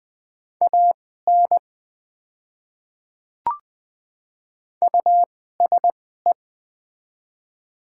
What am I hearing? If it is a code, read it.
Morse code, heard as a keyed tone, about 20 wpm, spaced as for 14 wpm, 700 Hz. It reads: AN USE
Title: 2x Character Spacing - Top 100 Words - 20wpm